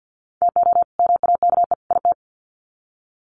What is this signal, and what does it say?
Morse code, heard as a keyed tone, about 50 wpm, 700 Hz. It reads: TO MAKE IT
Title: Sets of 3 Words 50wpm